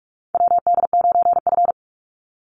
Morse code, heard as a keyed tone, about 44 wpm, 700 Hz. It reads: WD9F
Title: NA SOTA Chaser Callsigns 45wpm